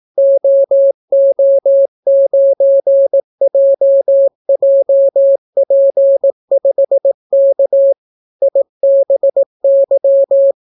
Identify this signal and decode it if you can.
Morse code, heard as a keyed tone, about 18 wpm, 550 Hz. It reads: OO9JJP5K IBY